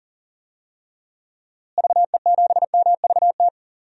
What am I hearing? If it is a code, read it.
Morse code, heard as a keyed tone, about 37 wpm, 700 Hz. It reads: VE7MVT
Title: DXCC-100 Call Signs - 6 Characters - Repeated 2x - 40wpm